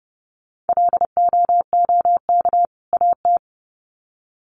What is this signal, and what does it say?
Morse code, heard as a keyed tone, about 30 wpm, 700 Hz. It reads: LOOK AT
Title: Sets of 2 Words 30wpm